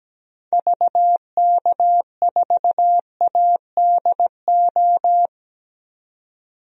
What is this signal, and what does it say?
Morse code, heard as a keyed tone, about 17 wpm, 700 Hz. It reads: VK4ADO
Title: DXCC-100 Call Signs - 6 Characters - 17wpm